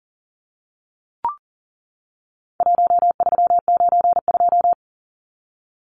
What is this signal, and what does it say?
Morse code, heard as a keyed tone, about 40 wpm, 700 Hz. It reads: 1392
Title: Numbers - Contest Style - 4-Digit 40wpm